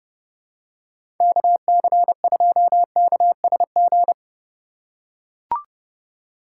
Morse code, heard as a keyed tone, about 30 wpm, 700 Hz. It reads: KC2KSG
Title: USA Call Signs - 6 Characters - 3x Character Spacing - 30wpm